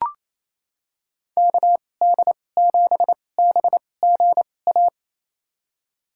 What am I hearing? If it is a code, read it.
Morse code, heard as a keyed tone, about 28 wpm, spaced as for 20 wpm, 700 Hz. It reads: KD7BGA
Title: USA Call Signs - 6 Characters - 2x Character Spacing - 28wpm